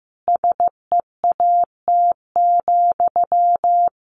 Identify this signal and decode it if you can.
Morse code, heard as a keyed tone, about 15 wpm, 700 Hz. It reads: SEAT,
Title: A Sound Of Thunder 15wpm